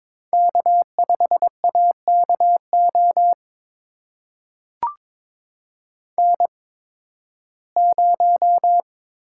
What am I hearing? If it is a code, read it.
Morse code, heard as a keyed tone, about 22 wpm, spaced as for 14 wpm, 700 Hz. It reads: K5AKO N0
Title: USA Call Signs - 8x Character Spacing - 22wpm